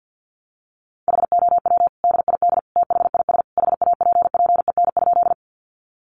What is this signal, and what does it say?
Morse code, heard as a keyed tone, about 50 wpm, 700 Hz. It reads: HOW DID THIS HAPPEN?
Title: Sentences from Top 400 Words 50wpm